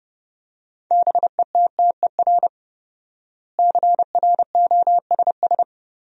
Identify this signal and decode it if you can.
Morse code, heard as a keyed tone, about 30 wpm, 700 Hz. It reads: BETTER CROSS
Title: 4x Word-Spacing - Sentences from Top 400 Words - 30wpm